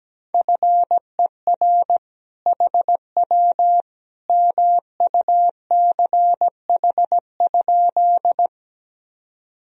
Morse code, heard as a keyed tone, about 17 wpm, 700 Hz. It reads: FER HW MUCH?
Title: Sentences from Top 200 Words and Abbreviations 17wpm